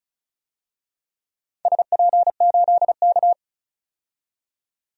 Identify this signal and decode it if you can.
Morse code, heard as a keyed tone, about 36 wpm, 690 Hz. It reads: SP8K
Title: DXCC-100 Call Signs - 4 Characters - 35wpm